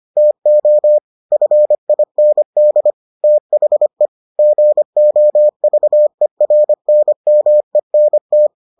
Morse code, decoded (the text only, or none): TO FIND THE GOVERNMENT